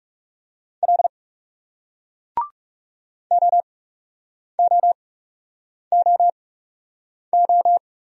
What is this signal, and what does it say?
Morse code, heard as a keyed tone, about 40 wpm, spaced as for 8 wpm, 700 Hz. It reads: L OOOO